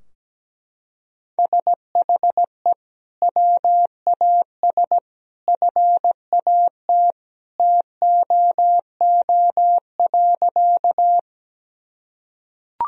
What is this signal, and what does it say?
Morse code, heard as a keyed tone, about 17 wpm, 700 Hz. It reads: SHE WAS FAT TOO.
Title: Sentences from Top 1000 Words - Review Part 4 - 17wpm